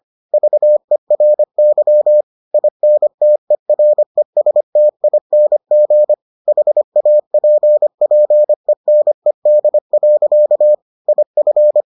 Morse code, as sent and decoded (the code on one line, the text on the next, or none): ...- . .-. -.-- / .. -. - . .-. ... - .. -. --. / .... .- .--. .--. -. -.. .-.-.- / .. ..-.
VERY INTERESTING HAPPENED. IF